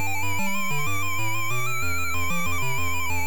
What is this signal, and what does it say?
There is no code.